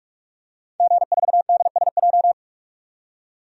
Morse code, heard as a keyed tone, about 41 wpm, 700 Hz. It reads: G4DSJ
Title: DXCC-100 Call Signs - 5 Characters - 45wpm